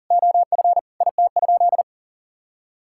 Morse code, heard as a keyed tone, about 37 wpm, 700 Hz. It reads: OF IT?
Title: Sentences from Top 1000 Words - Review Part 6 - 40wpm